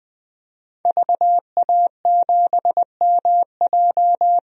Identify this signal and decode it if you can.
Morse code, heard as a keyed tone, about 20 wpm, 700 Hz. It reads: VA7MJ
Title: DXCC-100 Call Signs - 5 Characters - 20wpm